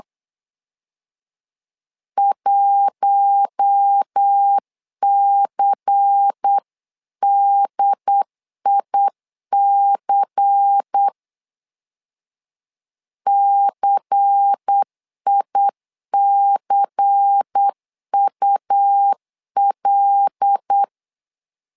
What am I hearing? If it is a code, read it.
Morse code, heard as a keyed tone, about 8 wpm, 800 Hz. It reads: 1CDIC CICUL
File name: LEZIONI / 5 LBVC / 5 LBVC 8wpm 2935